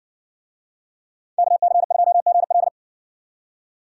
Morse code, heard as a keyed tone, about 60 wpm, 700 Hz. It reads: KC2GL